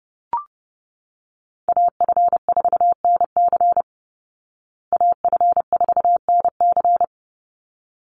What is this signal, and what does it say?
Morse code, heard as a keyed tone, about 30 wpm, 700 Hz. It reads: AF4NC AF4NC
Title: Call Signs Repeated 3x - 30wpm